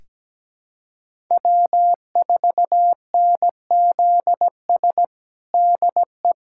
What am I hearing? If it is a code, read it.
Morse code, heard as a keyed tone, about 17 wpm, 700 Hz. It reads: W4NZS DE
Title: USA Callsign DE Callsign - 5 Characters - 17wpm